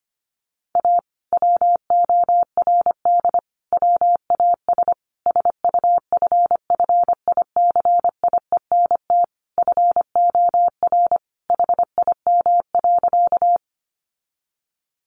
Morse code, heard as a keyed tone, about 25 wpm, 700 Hz. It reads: A WORD WAS SUFFICIENT FOR HIM.
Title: Sentences from Top 1400 Words 25wpm